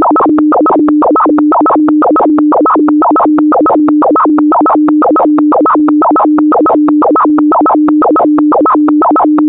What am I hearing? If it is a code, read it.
Two-tone teleprinter signal, about 400 Hz shift.